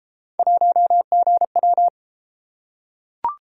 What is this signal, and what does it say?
Morse code, heard as a keyed tone, about 32 wpm, 700 Hz. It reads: 1GW